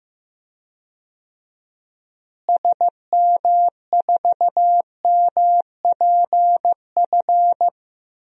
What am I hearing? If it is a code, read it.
Morse code, heard as a keyed tone, about 15 wpm, 700 Hz. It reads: SM4MPF
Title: DXCC-100 Call Signs - 6 Characters - Repeated 2x - 15wpm